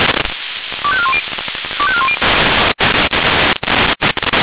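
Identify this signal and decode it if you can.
Five-tone selective call (ZVEI), heard as quick runs of five tones, tones of about 70 ms. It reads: 25410 25410
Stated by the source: ZVEI five-tone selective call received on SDR.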